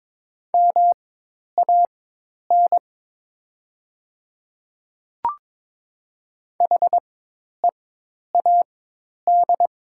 Morse code, heard as a keyed tone, about 22 wpm, spaced as for 10 wpm, 700 Hz. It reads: MAN HEAD
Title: 4x Character Spacing - Top 300 Words - 22wpm